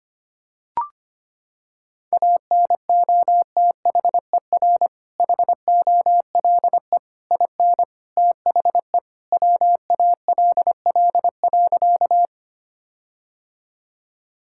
Morse code, heard as a keyed tone, about 25 wpm, 700 Hz. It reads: ANOTHER HOLE IN THE WALL.
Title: Sentences from Top 1600 Words 25wpm